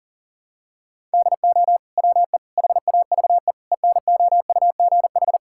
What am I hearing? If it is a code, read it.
Morse code, heard as a keyed tone, about 40 wpm, 700 Hz. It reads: DO WE HAVE ENOUGH
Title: Sentences from Top 1000 Words - Review Part 5 - 40wpm